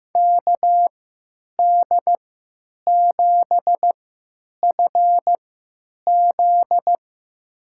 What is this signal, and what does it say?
Morse code, heard as a keyed tone, about 15 wpm, spaced as for 9 wpm, 700 Hz. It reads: KD7FZ
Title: USA Call Signs - 3x Character Spacing - 15wpm